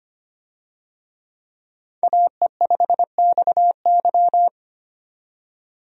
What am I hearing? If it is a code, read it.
Morse code, heard as a keyed tone, about 25 wpm, 700 Hz. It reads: AE5XY
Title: USA Call Signs - 5 Characters - 25wpm